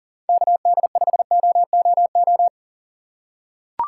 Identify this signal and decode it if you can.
Morse code, heard as a keyed tone, about 40 wpm, 700 Hz. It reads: KD5OOO
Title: USA Call Signs - 6 Characters - 4x Character Spacing - 40wpm